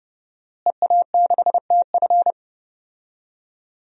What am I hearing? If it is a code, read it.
Morse code, heard as a keyed tone, about 31 wpm, 700 Hz. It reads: EA6TF
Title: DXCC-100 Call Signs - 5 Characters - Repeated 3x - 30wpm